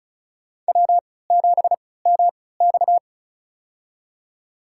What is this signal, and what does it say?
Morse code, heard as a keyed tone, about 36 wpm, spaced as for 20 wpm, 700 Hz. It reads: W7MX
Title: USA Call Signs - 4 Characters - 3x Character Spacing - 35wpm